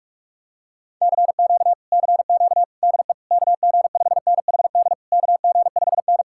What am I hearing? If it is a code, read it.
Morse code, heard as a keyed tone, about 46 wpm, 700 Hz. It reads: CQ CQ DE KG5NHD KG5N